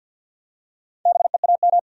Morse code, heard as a keyed tone, about 46 wpm, 700 Hz. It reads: BEAM
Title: Top 90 QSO Elements 50wpm